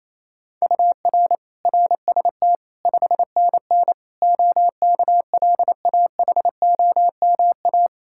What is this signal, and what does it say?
Morse code, heard as a keyed tone, about 28 wpm, 700 Hz. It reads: UR RST 5NN OKLAHOMA
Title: Practice QSO - Exchange 1 - Beginning - 28wpm